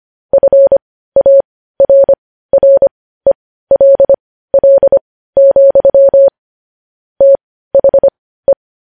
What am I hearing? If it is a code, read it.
Morse code, heard as a keyed tone, about 25 wpm, spaced as for 15 wpm, 550 Hz. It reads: FARRELL, THE
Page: News headlines in morse code at 15 wpm